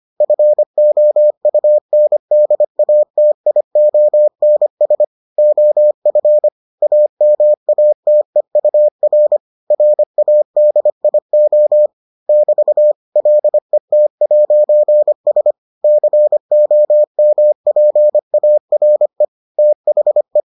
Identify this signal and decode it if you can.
Morse code, heard as a keyed tone, about 25 wpm, 600 Hz. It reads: FOUNDATIONS OF AMATEUR RADIO <BT> LET'S COMPARE THE